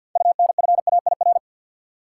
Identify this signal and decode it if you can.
Morse code, heard as a keyed tone, about 52 wpm, 700 Hz. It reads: UNFAIR
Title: Six Letter Words - 50wpm